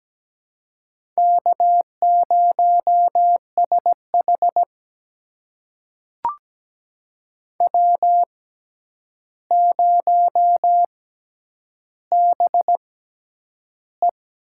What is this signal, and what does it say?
Morse code, heard as a keyed tone, about 17 wpm, spaced as for 9 wpm, 700 Hz. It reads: K0SH W0BE
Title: USA Call Signs - 4 Characters - 6x Character Spacing - 17wpm